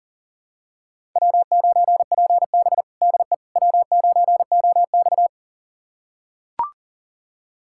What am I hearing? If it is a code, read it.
Morse code, heard as a keyed tone, about 40 wpm, 700 Hz. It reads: W9PB DE W9OX
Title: USA Callsign DE Callsign - 4 Characters - 40wpm